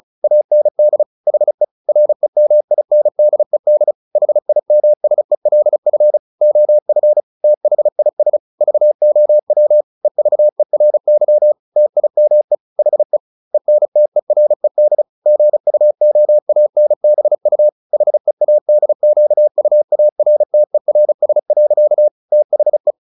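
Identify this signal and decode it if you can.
Morse code, heard as a keyed tone, about 34 wpm, 600 Hz. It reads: AND HE REMINDED HIMSELF OF THIS VOW EVERY TIME HE ENTERED GUOANBU HEADQUARTERS. THE
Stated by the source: Morse practice at 35 wpm from CW med Gnister, sending Ken Follett's novel Never in English.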